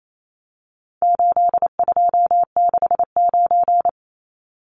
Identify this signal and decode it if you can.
Morse code, heard as a keyed tone, about 28 wpm, 700 Hz. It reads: 8269
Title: Numbers - 4-Digit 28wpm